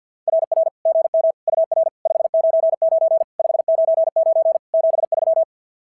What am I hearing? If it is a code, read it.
Morse code, heard as a keyed tone, about 45 wpm, 690 Hz. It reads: RR GM UR 599 599 73
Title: SOTA QSO Practice 50wpm